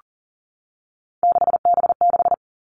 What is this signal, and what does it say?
Morse code, heard as a keyed tone, about 40 wpm, 700 Hz. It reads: 6B6
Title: All Cut Numbers - 3-Digit 40wpm